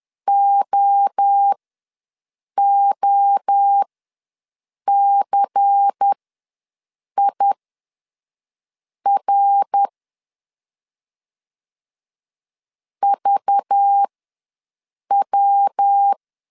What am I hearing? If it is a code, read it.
Morse code, heard as a keyed tone, about 10 wpm, spaced as for 6 wpm, 800 Hz. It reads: OOCIR VW
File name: LEZIONI / 5 LBVC / 5 LBVC ws 10wpm 3733